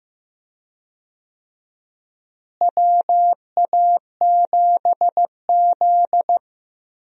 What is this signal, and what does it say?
Morse code, heard as a keyed tone, about 15 wpm, 700 Hz. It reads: WA7Z
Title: USA Call Signs - 4 Characters - Repeated 2x - 15wpm